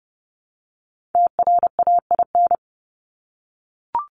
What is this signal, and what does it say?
Morse code, heard as a keyed tone, about 29 wpm, 700 Hz. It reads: TRAIN
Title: ING Suffix - Top 100 Words with Base Words 30wpm